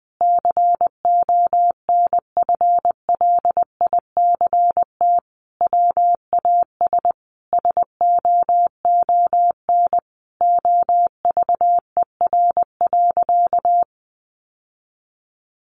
Morse code, heard as a keyed tone, about 19 wpm, 700 Hz. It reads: CONFLICT WAS SOON OVER.
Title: Sentences from Top 1300 Words 20wpm